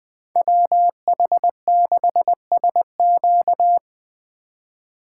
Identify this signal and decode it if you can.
Morse code, heard as a keyed tone, about 20 wpm, 700 Hz. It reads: WH6SQ
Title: Speed Racing - USA Call Signs 17wpm